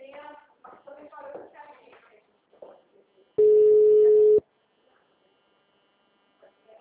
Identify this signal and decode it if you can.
Ringback tone (Europe, 425 Hz), with speech.
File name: out- -2018- - - .122